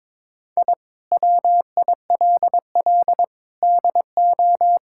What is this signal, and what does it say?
Morse code, heard as a keyed tone, about 22 wpm, 700 Hz. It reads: I WILL DO
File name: practice / 22wpm / Sets of 8 Words - 22wpm